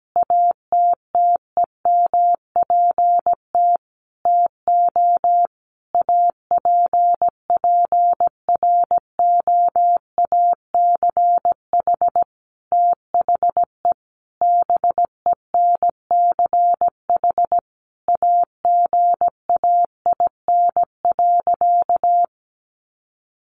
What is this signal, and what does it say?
Morse code, heard as a keyed tone, about 17 wpm, 700 Hz. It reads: ATTEMPT TO APPROACH THE BENCH AGAIN.
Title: Sentences from Top 1400 Words 17wpm